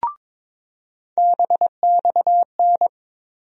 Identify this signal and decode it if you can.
Morse code, heard as a keyed tone, about 22 wpm, 700 Hz. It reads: BXN
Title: USA Call Sign Postfixes - 3 Letters 22wpm